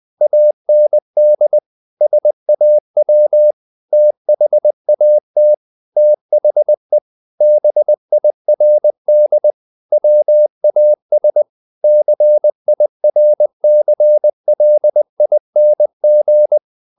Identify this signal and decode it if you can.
Morse code, heard as a keyed tone, about 20 wpm, 600 Hz. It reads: AND SAW THAT THE BIRD WAS CIRCLING
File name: the old man and the sea / the old man and the sea part 1 20wpm